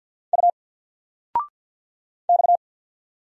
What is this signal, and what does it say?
Morse code, heard as a keyed tone, about 48 wpm, 700 Hz. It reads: U X
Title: Single Character (Letter:Number:Punctuation:Prosign) 50wpm